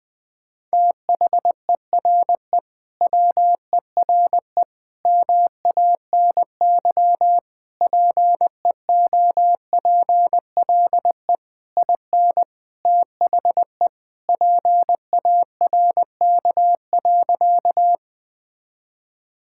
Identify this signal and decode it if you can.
Morse code, heard as a keyed tone, about 20 wpm, 700 Hz. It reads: THERE WERE MANY PEOPLE IN THE PARK.
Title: Sentences with 3-Word Phrases - 20wpm